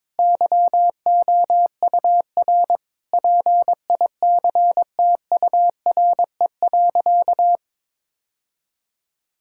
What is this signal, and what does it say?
Morse code, heard as a keyed tone, about 22 wpm, 700 Hz. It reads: YOUR PICTURE.
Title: Sentences from Top 1000 Words - Review Part 6 - 22wpm